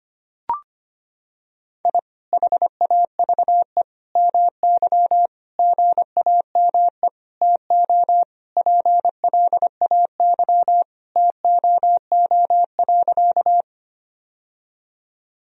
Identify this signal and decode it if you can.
Morse code, heard as a keyed tone, about 25 wpm, 700 Hz. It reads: I HAVE MY GAME TO PLAY TOO.
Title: Sentences from Top 1000 Words - Review Part 3 - 25wpm